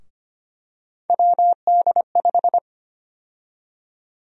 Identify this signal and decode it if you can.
Morse code, heard as a keyed tone, about 25 wpm, 700 Hz. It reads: WD5
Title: USA Call Sign Prefixes - 25wpm